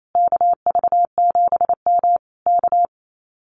Morse code, heard as a keyed tone, about 28 wpm, 700 Hz. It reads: KV7M K
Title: Common QSO Phrases Encore 28wpm